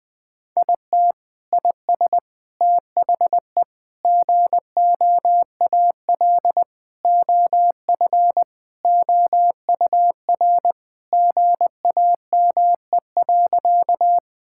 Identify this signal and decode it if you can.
Morse code, heard as a keyed tone, about 20 wpm, 700 Hz. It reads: IT IS THE GOAL OF OUR GAME.
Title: Sentences from Top 1000 Words - Review Part 5 - 20wpm